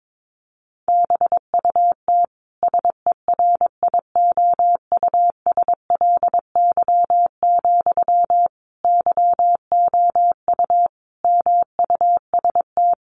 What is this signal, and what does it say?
Morse code, heard as a keyed tone, about 22 wpm, 700 Hz. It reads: BUT SERIOUSLY, YOU MUST
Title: Sentences from Top 1200 Words 22wpm